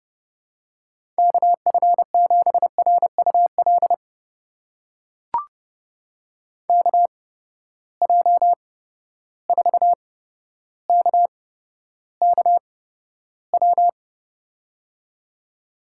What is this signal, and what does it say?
Morse code, heard as a keyed tone, about 30 wpm, spaced as for 13 wpm, 700 Hz. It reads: KF7RUL KJ4KKW
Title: USA Call Signs - 6 Characters - 8x Character Spacing - 30wpm